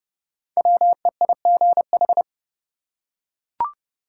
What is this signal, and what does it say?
Morse code, heard as a keyed tone, about 30 wpm, 700 Hz. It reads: WEIGH
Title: Five Letter Words - 30wpm